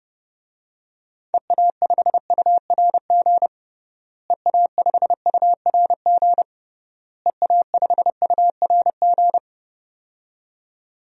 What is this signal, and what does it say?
Morse code, heard as a keyed tone, about 31 wpm, 700 Hz. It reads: EA5URG EA5URG EA5URG
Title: DXCC-100 Call Signs - 6 Characters - Repeated 3x - 30wpm